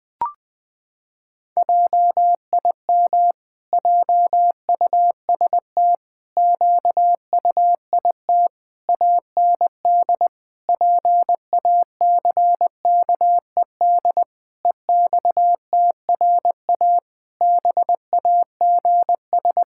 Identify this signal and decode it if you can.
Morse code, heard as a keyed tone, about 20 wpm, 700 Hz. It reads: JIM JUST QUIT AND PACKED EXTRA BAGS
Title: Pangram Challenge v2 - 20wpm